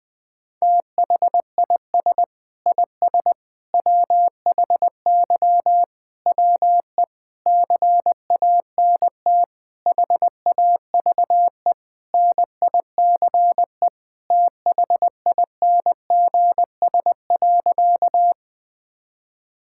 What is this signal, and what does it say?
Morse code, heard as a keyed tone, about 19 wpm, 700 Hz. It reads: THIS IS WHY WE CANT HAVE NICE THINGS.
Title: Sentences with 8-Word Phrases - 20wpm